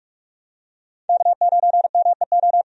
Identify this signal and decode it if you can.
Morse code, heard as a keyed tone, about 46 wpm, 700 Hz. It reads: K9MEO
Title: USA Call Signs - 5 Characters - 45wpm